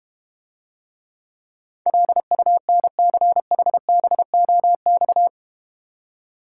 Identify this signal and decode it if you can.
Morse code, heard as a keyed tone, about 31 wpm, 700 Hz. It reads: LUNCHBOX
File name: Easy Words - Speed-Racing with Hints - 25wpm